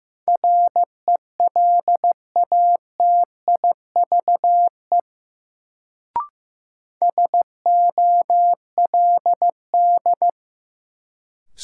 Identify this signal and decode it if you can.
Morse code, heard as a keyed tone, about 15 wpm, 700 Hz. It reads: RELATIVE SOLD
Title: Sentences from Top 1400 Words - New Word Drill Alt - 15wpm